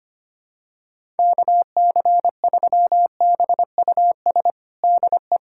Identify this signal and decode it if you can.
Morse code, heard as a keyed tone, about 25 wpm, 700 Hz. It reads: KC3BUS DE
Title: USA Callsign DE Callsign 25wpm